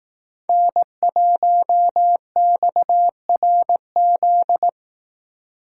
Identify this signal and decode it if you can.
Morse code, heard as a keyed tone, about 18 wpm, 700 Hz. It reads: N1XRZ
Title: Speed Racing - USA 1x3 Call Signs 15wpm